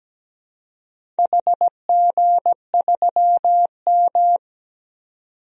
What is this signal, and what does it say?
Morse code, heard as a keyed tone, about 17 wpm, 700 Hz. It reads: HG3M
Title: DXCC-100 Call Signs - 4 Characters - 17wpm